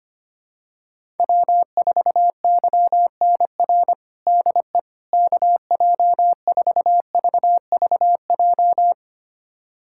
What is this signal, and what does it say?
Morse code, heard as a keyed tone, about 25 wpm, 700 Hz. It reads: W4YNR DE KJ4VVJ
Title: USA Callsign DE Callsign 25wpm